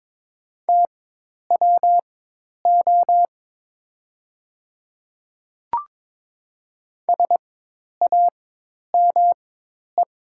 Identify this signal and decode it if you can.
Morse code, heard as a keyed tone, about 22 wpm, spaced as for 10 wpm, 700 Hz. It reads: TWO SAME
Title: 4x Character Spacing - Top 300 Words - 22wpm